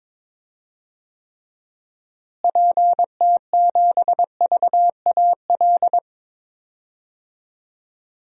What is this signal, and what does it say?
Morse code, heard as a keyed tone, about 22 wpm, 700 Hz. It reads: PT7VAL
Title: DXCC-100 Call Signs - 6 Characters - Repeated 3x - 22wpm